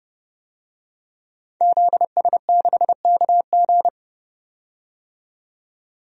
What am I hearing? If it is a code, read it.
Morse code, heard as a keyed tone, about 31 wpm, 700 Hz. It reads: ZS6KG
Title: DXCC-100 Call Signs - 5 Characters - 30wpm